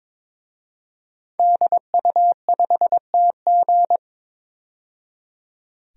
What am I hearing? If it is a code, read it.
Morse code, heard as a keyed tone, about 22 wpm, 700 Hz. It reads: DU5TG
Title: DXCC-100 Call Signs - 5 Characters - 22wpm